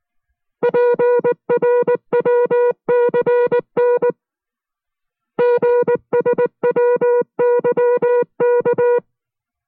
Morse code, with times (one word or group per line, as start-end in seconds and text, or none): 0.6-4.1 PRWCN
5.4-9.0 GSWYK